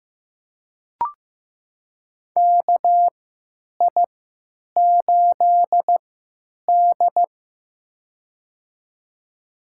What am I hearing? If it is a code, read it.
Morse code, heard as a keyed tone, about 15 wpm, spaced as for 9 wpm, 700 Hz. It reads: KI8D
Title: USA Call Signs - 4 Characters - 3x Character Spacing - 15wpm